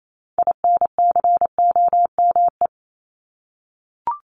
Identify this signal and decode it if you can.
Morse code, heard as a keyed tone, about 28 wpm, 700 Hz. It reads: INCOME
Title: Top 751-1000 Words 28wpm